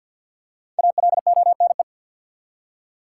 Morse code, heard as a keyed tone, about 51 wpm, 700 Hz. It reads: ALONE
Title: Five Letter Words - 50wpm